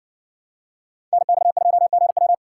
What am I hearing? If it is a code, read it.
Morse code, heard as a keyed tone, about 55 wpm, 700 Hz. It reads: NX2GW